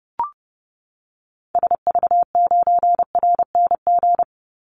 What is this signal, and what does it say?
Morse code, heard as a keyed tone, about 30 wpm, 700 Hz. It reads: SV9RNG
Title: Call Signs Repeated 3x - 30wpm